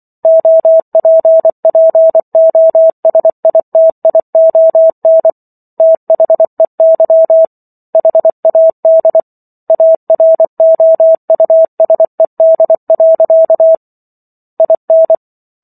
Morse code, heard as a keyed tone, about 24 wpm, 650 Hz. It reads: OPPOSITION THEY HAD AROUSED. IN